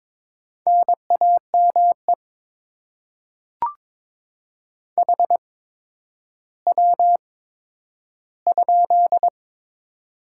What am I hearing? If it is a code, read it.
Morse code, heard as a keyed tone, about 22 wpm, spaced as for 11 wpm, 700 Hz. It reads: NAME HW?
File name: Top 20 QSO Elements - 8x Character Spacing - 22wpm